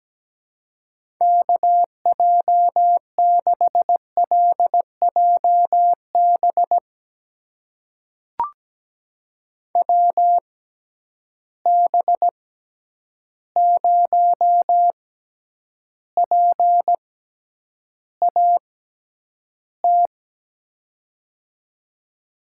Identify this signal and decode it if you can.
Morse code, heard as a keyed tone, about 17 wpm, spaced as for 9 wpm, 700 Hz. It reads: KJ6LJB WB0PAT
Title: USA Call Signs - 6 Characters - 6x Character Spacing - 17wpm